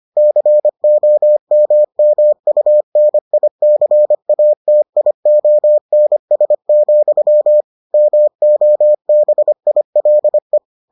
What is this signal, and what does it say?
Morse code, heard as a keyed tone, about 25 wpm, 600 Hz. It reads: COMMUNICATIONS, MOBILE